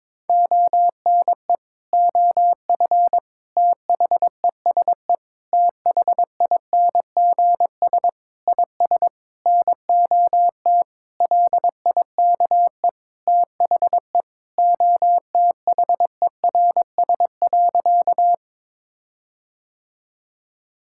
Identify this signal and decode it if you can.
Morse code, heard as a keyed tone, about 22 wpm, 700 Hz. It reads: ONE OF THESE THINGS IS NOT LIKE THE OTHERS.
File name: practice / 22wpm / Sentences with 8-Word Phrases - 22wpm